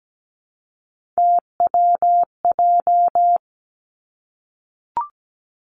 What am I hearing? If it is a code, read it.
Morse code, heard as a keyed tone, about 17 wpm, 700 Hz. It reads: TWJ
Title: Three Challenging Letter-Number Combinations 17wpm